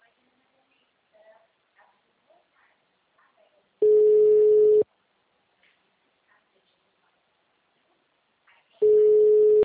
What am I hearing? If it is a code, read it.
Ringback tone (Europe, 425 Hz).